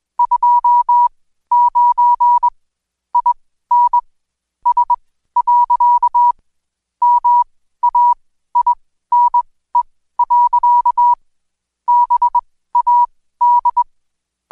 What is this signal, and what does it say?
Morse code, heard as a keyed tone, about 20 wpm, spaced as for 14 wpm, 970 Hz. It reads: 29 IN S. MAINE. BAD
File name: Code Quick / AudioDownld2 / Seg123-Day28-DayDrill 28